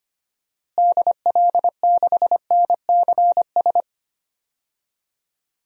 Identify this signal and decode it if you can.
Morse code, heard as a keyed tone, about 25 wpm, 700 Hz. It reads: DL6NCS
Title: DXCC-100 Call Signs - 6 Characters - Repeated 2x - 25wpm